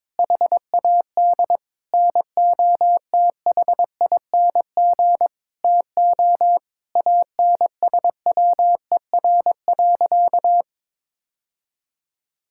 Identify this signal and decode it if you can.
Morse code, heard as a keyed tone, about 22 wpm, 700 Hz. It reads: HAD NOTHING TO ANSWER.
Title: Sentences from Top 1600 Words 22wpm